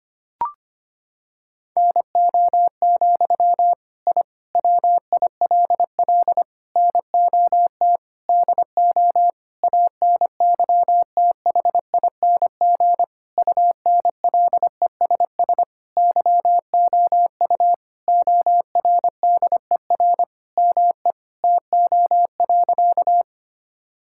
Morse code, heard as a keyed tone, about 25 wpm, 700 Hz. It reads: NO, I WILL NOT DO ANYTHING UNLESS YOU ORDER ME TO.
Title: Sentences from Top 1000 Words - Review Part 6 - 25wpm